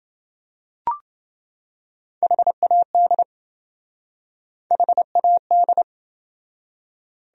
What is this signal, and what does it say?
Morse code, heard as a keyed tone, about 28 wpm, 700 Hz. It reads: HAD HAD